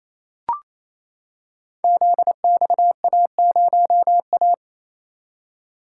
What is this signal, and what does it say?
Morse code, heard as a keyed tone, about 27 wpm, 700 Hz. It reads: ZXA0A
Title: Code Groups - 5 Letter:Numbers - 28wpm